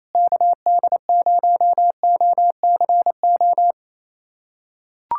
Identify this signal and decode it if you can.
Morse code, heard as a keyed tone, about 28 wpm, 700 Hz. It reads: KD0OCO
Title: USA Callsign DE Callsign - 6 Characters - 28wpm